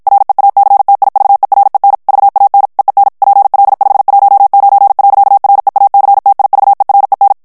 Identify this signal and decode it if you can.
Morse code, heard as a keyed tone, about 52 wpm, 800 Hz. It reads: REACTIVEREN VAN EEN PL519. REACTIVEREN
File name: morse mp3 / HOLLANDAIS texte clair 2 vitesse 50 mots minute